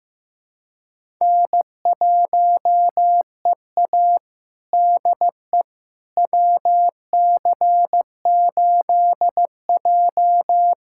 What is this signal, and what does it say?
Morse code, heard as a keyed tone, about 15 wpm, 700 Hz. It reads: N1EA DE WC8J
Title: USA Callsign DE Callsign - 4 Characters - 15wpm